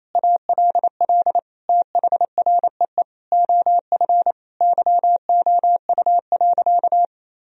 Morse code, heard as a keyed tone, about 27 wpm, 700 Hz. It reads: ALL THREE OF YOU.
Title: Sentences from Top 1000 Words - Review Part 3 - 28wpm